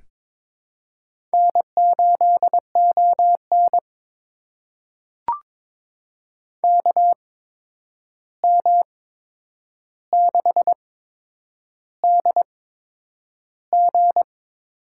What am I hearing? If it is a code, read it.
Morse code, heard as a keyed tone, about 22 wpm, spaced as for 9 wpm, 700 Hz. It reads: N8ON KM6DG
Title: USA Call Signs - 8x Character Spacing - 22wpm